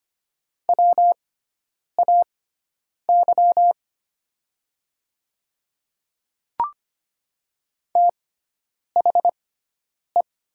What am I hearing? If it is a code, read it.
Morse code, heard as a keyed tone, about 25 wpm, spaced as for 9 wpm, 700 Hz. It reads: WAY THE
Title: 6x Character Spacing - Top 100 Words - 25wpm